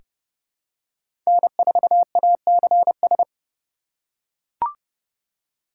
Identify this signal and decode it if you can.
Morse code, heard as a keyed tone, about 32 wpm, 700 Hz. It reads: N4ACS
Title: USA Call Signs - 5 Characters - 2x Character Spacing - 30wpm